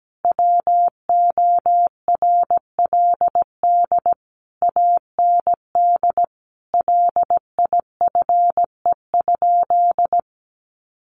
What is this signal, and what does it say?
Morse code, heard as a keyed tone, about 17 wpm, 700 Hz. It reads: WORLD AND LIFE?
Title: Sentences from Top 1300 Words 17wpm